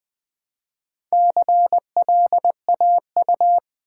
Morse code, heard as a keyed tone, about 20 wpm, 700 Hz. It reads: CLAU
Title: Build A Word - 2 Letters At A Time - 6-Letter Words - 20wpm